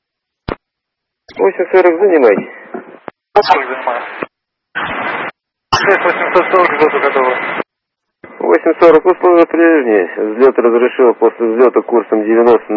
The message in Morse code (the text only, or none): none